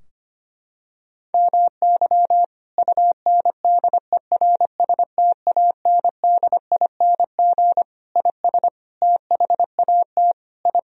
Morse code, heard as a keyed tone, about 25 wpm, 700 Hz. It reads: MY UNDERSTANDING IS THAT I
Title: Sentences from Top 1000 Words - Review Part 3 - 25wpm